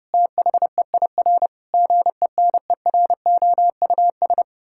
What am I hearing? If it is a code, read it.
Morse code, heard as a keyed tone, about 30 wpm, 700 Hz. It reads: THEIR GENEROUS
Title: Pangram Challenge v2 - 30wpm